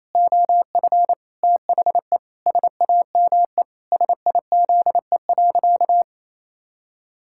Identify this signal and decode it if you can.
Morse code, heard as a keyed tone, about 28 wpm, 700 Hz. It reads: OF THE SAME SIZE.
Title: Pangram Challenge v2 - 28wpm